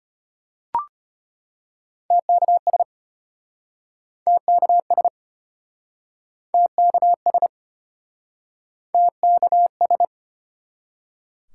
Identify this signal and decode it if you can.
Morse code, heard as a keyed tone, about 33 wpm, 700 Hz. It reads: TKS TKS TKS TKS